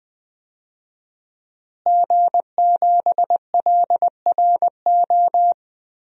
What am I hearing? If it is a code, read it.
Morse code, heard as a keyed tone, about 20 wpm, 700 Hz. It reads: G7LRO
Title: DXCC-100 Call Signs - 5 Characters - Repeated 3x - 20wpm